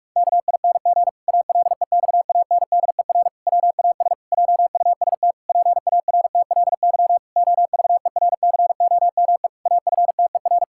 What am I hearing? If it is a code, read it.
Morse code, heard as a keyed tone, about 46 wpm, 700 Hz. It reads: KING ALEXANDER WAS JUST PARTLY OVERCOME AFTER
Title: Pangram Challenge v2 - 45wpm